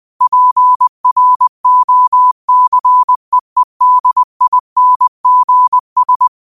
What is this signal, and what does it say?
Morse code, heard as a keyed tone, about 20 wpm, 1000 Hz. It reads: PROCEEDINGS
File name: qotdfiles / 20wpm